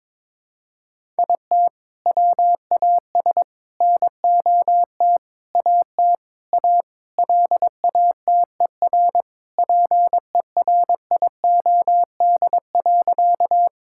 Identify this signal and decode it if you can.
Morse code, heard as a keyed tone, about 22 wpm, 700 Hz. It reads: IT WAS NOT AT A LATER PERIOD.
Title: Sentences from Top 1000 Words - Review Part 5 - 22wpm